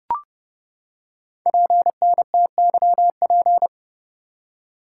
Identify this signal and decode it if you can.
Morse code, heard as a keyed tone, about 30 wpm, 700 Hz. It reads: PNTYP
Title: Code Groups - 5 Letter:Numbers - 30wpm